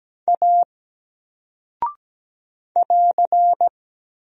Morse code, heard as a keyed tone, about 17 wpm, 700 Hz. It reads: A <AR>
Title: Single Character (Letter:Number:Punctuation:Prosign) Alt 17wpm